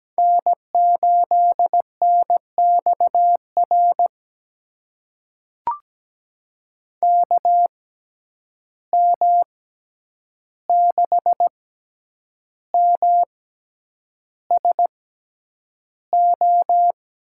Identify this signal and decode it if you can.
Morse code, heard as a keyed tone, about 17 wpm, spaced as for 8 wpm, 700 Hz. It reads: N8NXR KM6MSO
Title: USA Call Signs - 6x Character Spacing - 17wpm